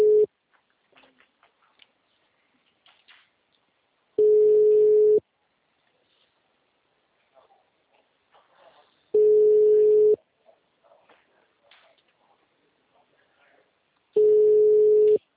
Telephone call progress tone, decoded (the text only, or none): ringback tone